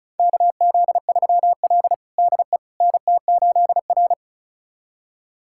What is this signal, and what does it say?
Morse code, heard as a keyed tone, about 34 wpm, 700 Hz. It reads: KZ3L DE NT8R
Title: USA Callsign DE Callsign - 4 Characters - 35wpm